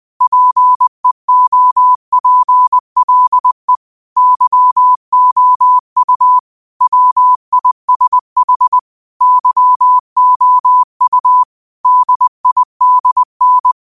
Morse code, heard as a keyed tone, about 20 wpm, 1000 Hz. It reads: PEOPLE YOU WISH YOU DIDN